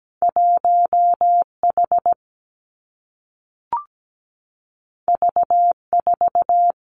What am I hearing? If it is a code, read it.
Morse code, heard as a keyed tone, about 17 wpm, 700 Hz. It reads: 1H V4
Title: Two Challenging Letter-Number Combinations 17wpm